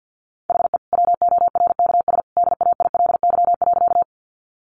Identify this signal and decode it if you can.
Morse code, heard as a keyed tone, about 50 wpm, 700 Hz. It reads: HE WORKS DAILY.
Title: Sentences from Top 1100 Words 50wpm